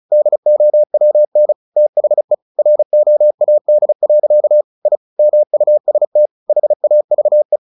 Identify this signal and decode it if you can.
Morse code, heard as a keyed tone, about 35 wpm, 600 Hz. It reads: DOWN THE ROAD. I MUST HAVE
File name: the old man and the sea / the old man and the sea 1 35wpm